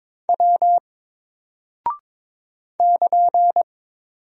Morse code, heard as a keyed tone, about 23 wpm, 700 Hz. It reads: W <KN>
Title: Single Character (Letter:Number:Punctuation:Prosign) 22wpm